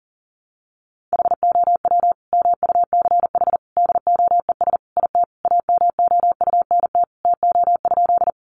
Morse code, heard as a keyed tone, about 40 wpm, 700 Hz. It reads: HOW MUCH DOES IT AMOUNT TO?
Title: Sentences from Top 700 Words 40wpm